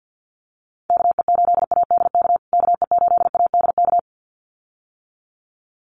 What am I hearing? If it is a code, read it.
Morse code, heard as a keyed tone, about 50 wpm, 700 Hz. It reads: KE8ADK KE8ADK
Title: USA Call Signs Repeated 2x 50wpm